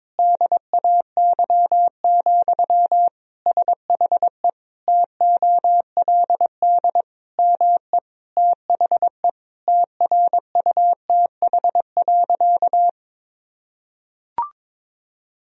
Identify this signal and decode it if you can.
Morse code, heard as a keyed tone, about 22 wpm, 700 Hz. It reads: DAY, SHE TOLD ME THE TRUTH.
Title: Sentences from Top 1000 Words - Review Part 4 - 22wpm